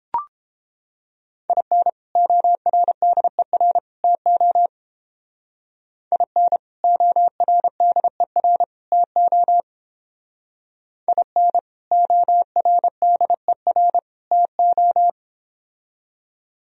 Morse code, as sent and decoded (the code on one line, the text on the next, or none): .. -. / --- .-. -.. .-. / - --- / .. -. / --- .-. -.. .-. / - --- / .. -. / --- .-. -.. .-. / - ---
IN ORDER TO IN ORDER TO IN ORDER TO